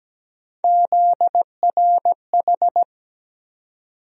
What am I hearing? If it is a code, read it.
Morse code, heard as a keyed tone, about 17 wpm, 700 Hz. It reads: ZRH
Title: USA Call Sign Postfixes - 3 Letters 17wpm